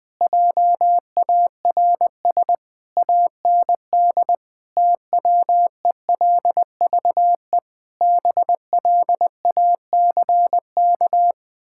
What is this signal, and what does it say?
Morse code, heard as a keyed tone, about 20 wpm, 700 Hz. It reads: JARS AND TWELVE BLACK
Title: Pangram Challenge v2 - 20wpm